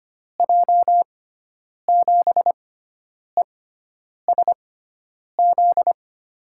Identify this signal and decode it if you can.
Morse code, heard as a keyed tone, about 25 wpm, spaced as for 9 wpm, 700 Hz. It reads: J7ESZ
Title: USA Call Signs - 6 Characters - 6x Character Spacing - 25wpm